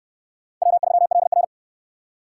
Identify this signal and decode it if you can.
Morse code, heard as a keyed tone, about 68 wpm, 700 Hz. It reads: W3LR